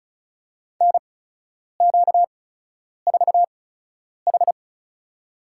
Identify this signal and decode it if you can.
Morse code, heard as a keyed tone, about 36 wpm, spaced as for 10 wpm, 700 Hz. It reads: NQ4H
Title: USA Call Signs - 8x Character Spacing - 35wpm